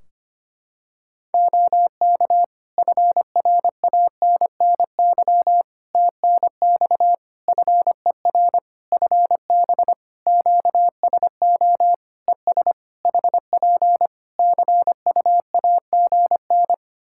Morse code, heard as a keyed tone, about 25 wpm, 700 Hz. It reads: OK FRANNY TNX FER FB QSO ES HP CUAGN
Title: Practice QSO - Exchange 4 - Ending - 25wpm